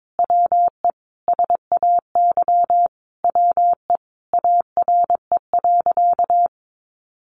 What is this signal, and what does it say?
Morse code, heard as a keyed tone, about 22 wpm, 700 Hz. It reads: WE SAY WE ARE.
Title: Sentences from Top 200 Words 22wpm